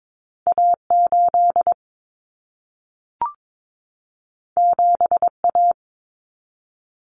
Morse code, heard as a keyed tone, about 22 wpm, 700 Hz. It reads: A8 7A
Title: Common Cut Numbers - 2-Digit 22wpm